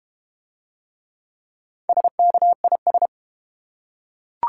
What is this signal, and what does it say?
Morse code, heard as a keyed tone, about 30 wpm, 700 Hz. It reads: SKIS
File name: Easy Words - Speed-Racing with Hints - 25wpm